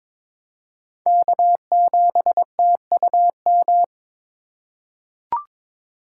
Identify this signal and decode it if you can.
Morse code, heard as a keyed tone, about 22 wpm, 700 Hz. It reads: K7TUM
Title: USA Call Signs - 5 Characters - 3x Character Spacing - 22wpm